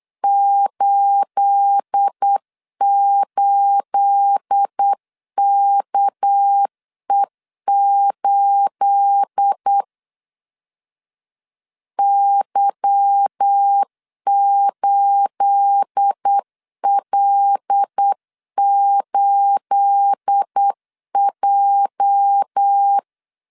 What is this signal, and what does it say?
Morse code, heard as a keyed tone, about 8 wpm, 800 Hz. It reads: 88KE8 Y8L8J
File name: LEZIONI / 9 28 8wpm 3512